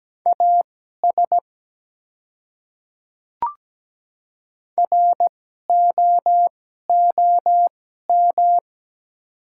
Morse code, heard as a keyed tone, about 17 wpm, spaced as for 12 wpm, 700 Hz. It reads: AS ROOM
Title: 2x Character Spacing - Top 300 Words - 17wpm